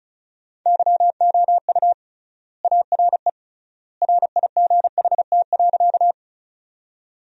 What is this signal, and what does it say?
Morse code, heard as a keyed tone, about 36 wpm, 700 Hz. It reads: YOU ARE RIGHT.
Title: 3x Word-Spacing - Sentences from Top 200 Words - 35wpm